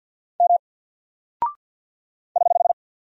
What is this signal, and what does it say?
Morse code, heard as a keyed tone, about 48 wpm, 700 Hz. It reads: M <HH>
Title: Single Character (Letter:Number:Punctuation:Prosign) Alt 50wpm